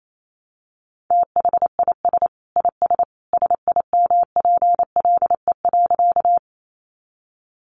Morse code, heard as a keyed tone, about 28 wpm, 700 Hz. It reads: THIS IS SIMPLE.
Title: Sentences from Top 400 Words 28wpm